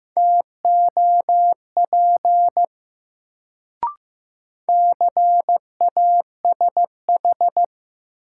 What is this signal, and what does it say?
Morse code, heard as a keyed tone, about 15 wpm, 700 Hz. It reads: TOP CASH
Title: Sentences from Top 700 Words - New Word Drill - 15wpm